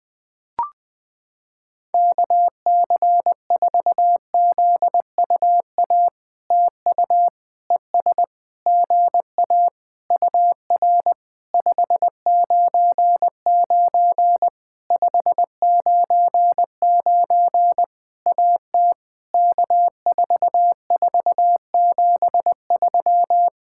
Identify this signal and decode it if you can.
Morse code, heard as a keyed tone, about 20 wpm, 700 Hz. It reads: KC4ZUA TU ES GA UR 599 599 AT K4473
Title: POTA QSO Practice - 20wpm